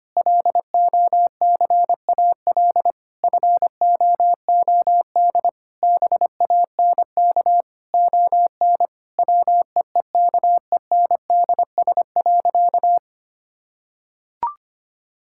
Morse code, heard as a keyed tone, about 25 wpm, 700 Hz. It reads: LOCAL FOOD BANK ON WEEKENDS.
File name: practice / 25wpm / Difficult Sentences with Hints - 25wpm